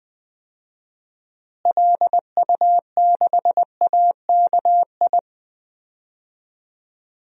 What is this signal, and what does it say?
Morse code, heard as a keyed tone, about 20 wpm, 700 Hz. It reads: LU6AKI
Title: DXCC-100 Call Signs - 6 Characters - Repeated 3x - 20wpm